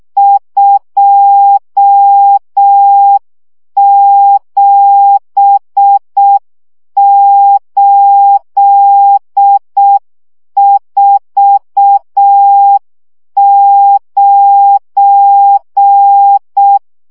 Morse code, heard as a keyed tone, about 6 wpm, 800 Hz. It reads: 27849
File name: morze vizsga 2011-2 5